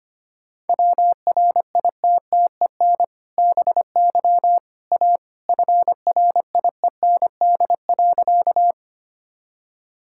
Morse code, heard as a keyed tone, about 25 wpm, 700 Hz. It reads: WRITTEN BY A FRIEND.
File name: practice / 25wpm / Sentences with 2-Word Phrases - 25wpm